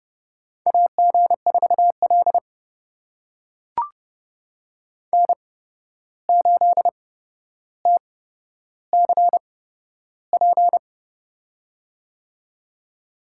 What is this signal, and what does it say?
Morse code, heard as a keyed tone, about 30 wpm, spaced as for 12 wpm, 700 Hz. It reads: AG4L N8TCP
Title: USA Call Signs - 8x Character Spacing - 30wpm